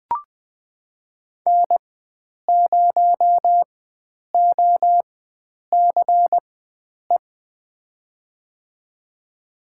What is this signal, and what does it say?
Morse code, heard as a keyed tone, about 21 wpm, spaced as for 9 wpm, 700 Hz. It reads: N0OCE
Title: USA Call Signs - 4x Character Spacing - 20wpm